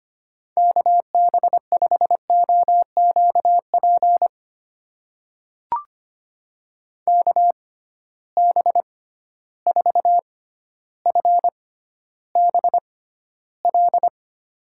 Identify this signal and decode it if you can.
Morse code, heard as a keyed tone, about 25 wpm, spaced as for 13 wpm, 700 Hz. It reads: KB5OQP KB4FBL
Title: USA Call Signs - 6 Characters - 6x Character Spacing - 25wpm